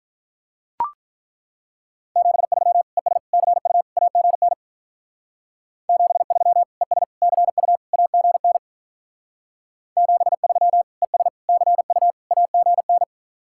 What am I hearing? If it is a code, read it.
Morse code, heard as a keyed tone, about 43 wpm, 700 Hz. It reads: 73 ES CU AGN 73 ES CU AGN 73 ES CU AGN
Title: Speed-Racing Common QSO Phrases Encore - 35wpm